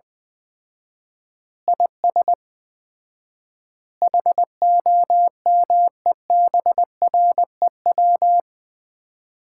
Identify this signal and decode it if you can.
Morse code, heard as a keyed tone, about 20 wpm, 700 Hz. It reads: IS HOMEBREW
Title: Common QSO Phrases - 4x Word-Spacing - 20wpm